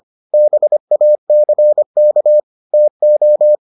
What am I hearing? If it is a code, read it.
Morse code, heard as a keyed tone, about 25 wpm, 600 Hz. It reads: BACK TO